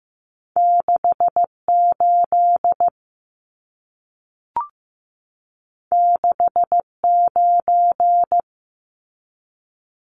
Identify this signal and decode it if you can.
Morse code, heard as a keyed tone, about 15 wpm, 700 Hz. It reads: 68 69
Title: Numbers - 2-Digit 15wpm